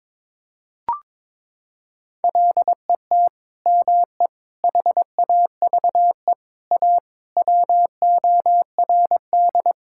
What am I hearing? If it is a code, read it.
Morse code, heard as a keyed tone, about 22 wpm, 700 Hz. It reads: LET ME HAVE A WORD
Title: Sentences from Top 1000 Words - Review Part 6 - 22wpm